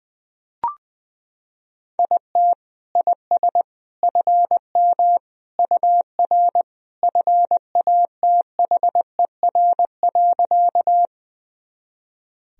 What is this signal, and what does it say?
Morse code, heard as a keyed tone, about 20 wpm, 700 Hz. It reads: IT IS FM UR FATHER.
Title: Sentences from Top 300 Words and Abbreviations 20wpm